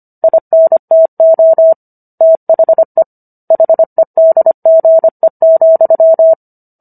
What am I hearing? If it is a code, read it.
Morse code, heard as a keyed tone, about 25 wpm, 650 Hz. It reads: INTO THE HEDGE,